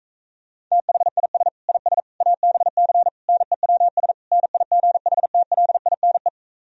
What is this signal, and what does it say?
Morse code, heard as a keyed tone, about 45 wpm, 700 Hz. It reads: THIS IS ABC NEWS NIGHTLINE